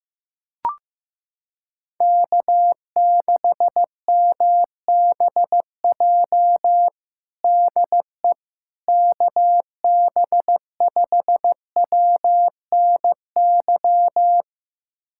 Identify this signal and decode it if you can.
Morse code, heard as a keyed tone, about 15 wpm, 700 Hz. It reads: K6MBJ DE KB5WNY
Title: USA Callsign DE Callsign 15wpm